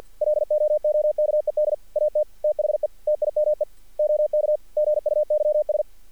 Morse code, heard as a keyed tone, about 50 wpm, 600 Hz. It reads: LOOKED AT THE TIME OK GUYS
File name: archived / CRSnet